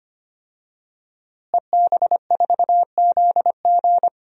Morse code, heard as a keyed tone, about 25 wpm, 700 Hz. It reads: EB4ZG
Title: DXCC-100 Call Signs - 5 Characters - Repeated 3x - 25wpm